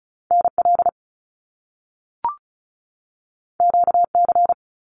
Morse code, heard as a keyed tone, about 36 wpm, 700 Hz. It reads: NL QC
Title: Canadian Provinces and Territories 35wpm